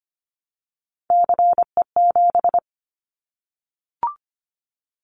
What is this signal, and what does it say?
Morse code, heard as a keyed tone, about 25 wpm, 700 Hz. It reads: CE7
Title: Callsign Prefixes for Top 100 DXCC - 25wpm